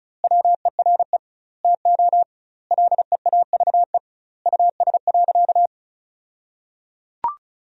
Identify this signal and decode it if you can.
Morse code, heard as a keyed tone, about 36 wpm, 700 Hz. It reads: WERE TO LEAVE US.
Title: 2x Word-Spacing - Sentences from Top 400 Words - 35wpm